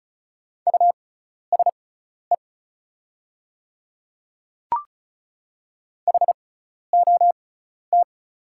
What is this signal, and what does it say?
Morse code, heard as a keyed tone, about 36 wpm, spaced as for 12 wpm, 690 Hz. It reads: USE HOT
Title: 6x Character Spacing - Top 100 Words - 35wpm